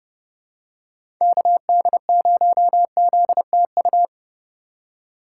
Morse code, heard as a keyed tone, about 32 wpm, 700 Hz. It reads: KD0ZTU
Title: USA Call Signs - 3x Character Spacing - 30wpm